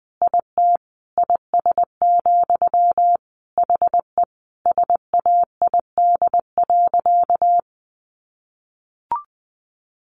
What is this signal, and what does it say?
Morse code, heard as a keyed tone, about 20 wpm, 700 Hz. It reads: IT IS, HE SAID.
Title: Sentences from Top 200 Words 20wpm